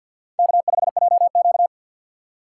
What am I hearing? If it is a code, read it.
Morse code, heard as a keyed tone, about 53 wpm, 690 Hz. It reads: K5JQ
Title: USA Call Signs - 4x Character Spacing - 50wpm